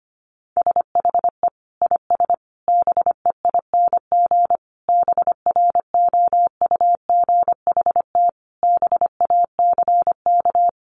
Morse code, heard as a keyed tone, about 25 wpm, 700 Hz. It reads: SHE IS BEING BROUGHT BACK
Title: Sentences from Top 600 Words 25wpm